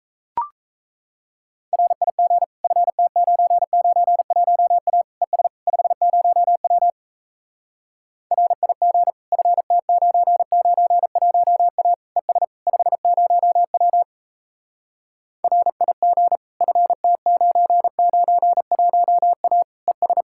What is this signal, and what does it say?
Morse code, heard as a keyed tone, about 38 wpm, 700 Hz. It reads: RIG FT991A ES 50W RIG FT991A ES 50W RIG FT991A ES